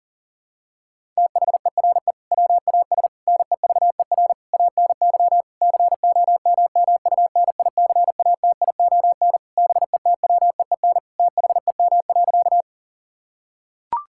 Morse code, heard as a keyed tone, about 40 wpm, 700 Hz. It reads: THERE WAS NEVER ANY COMMUNICATION BETWEEN THEM.
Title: Sentences from Top 1000 Words - Review Part 6 - 40wpm